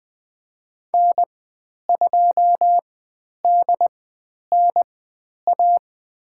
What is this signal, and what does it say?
Morse code, heard as a keyed tone, about 20 wpm, spaced as for 10 wpm, 700 Hz. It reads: N2DNA
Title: 109 Lesson 18g - Letter m - Call Signs 20wpm